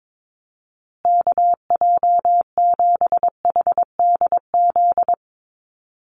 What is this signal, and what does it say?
Morse code, heard as a keyed tone, about 22 wpm, 700 Hz. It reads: KJ7HDZ
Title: USA Call Signs Repeated 3x 22wpm